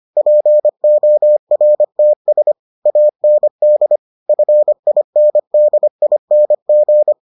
Morse code, heard as a keyed tone, about 25 wpm, 600 Hz. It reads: PORTS AND FINDING